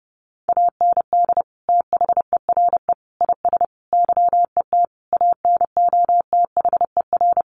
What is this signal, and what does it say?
Morse code, heard as a keyed tone, about 28 wpm, 700 Hz. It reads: AND THERE IS YET ANOTHER
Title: Sentences from Top 1500 Words 30wpm